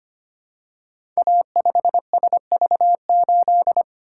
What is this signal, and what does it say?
Morse code, heard as a keyed tone, about 25 wpm, 700 Hz. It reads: A5SV8
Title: Code Groups - 5 Letter:Numbers - 25wpm